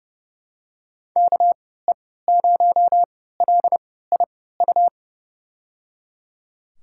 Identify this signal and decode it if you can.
Morse code, heard as a keyed tone, about 30 wpm, spaced as for 17 wpm, 700 Hz. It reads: KE0LIU
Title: USA Call Signs - 6 Characters - 3x Character Spacing - 30wpm